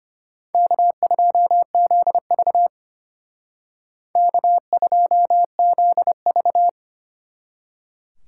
Morse code, heard as a keyed tone, about 32 wpm, 700 Hz. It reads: K2ZV K2ZV